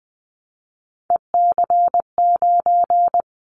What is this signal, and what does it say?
Morse code, heard as a keyed tone, about 20 wpm, 700 Hz. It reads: EC9
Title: Callsign Prefixes for Top 100 DXCC - 20wpm